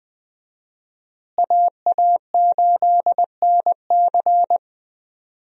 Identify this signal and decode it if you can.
Morse code, heard as a keyed tone, about 20 wpm, 700 Hz. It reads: AA8NC
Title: USA Call Signs - 5 Characters - Repeated 3x - 20wpm